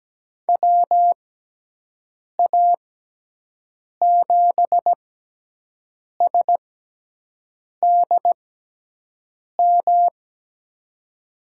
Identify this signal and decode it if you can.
Morse code, heard as a keyed tone, about 17 wpm, spaced as for 6 wpm, 700 Hz. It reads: WA7SDM
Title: USA Call Signs - 6 Characters - 6x Character Spacing - 17wpm